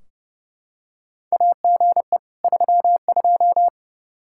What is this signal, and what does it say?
Morse code, heard as a keyed tone, about 30 wpm, 700 Hz. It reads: AGE 32
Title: K1USN SST QSO Practice 30wpm